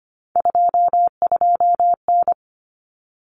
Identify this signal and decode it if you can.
Morse code, heard as a keyed tone, about 25 wpm, 700 Hz. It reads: 22N
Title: Common Cut Numbers - 3-Digit 25wpm